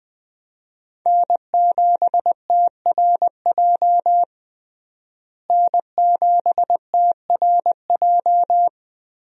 Morse code, heard as a keyed tone, about 20 wpm, 700 Hz. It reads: N7TRJ N7TRJ
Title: USA Call Signs - 5 Characters - Repeated 3x - 20wpm